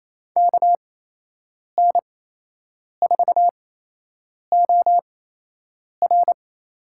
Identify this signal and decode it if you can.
Morse code, heard as a keyed tone, about 28 wpm, spaced as for 8 wpm, 700 Hz. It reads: KN4OR
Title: USA Call Signs - 6 Characters - 8x Character Spacing - 28wpm